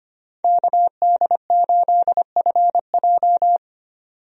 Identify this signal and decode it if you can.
Morse code, heard as a keyed tone, about 25 wpm, 690 Hz. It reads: KD8FJ
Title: USA Call Signs - 5 Characters - 25wpm